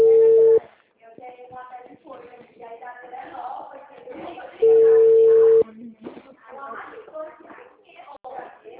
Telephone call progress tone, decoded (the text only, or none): ringback tone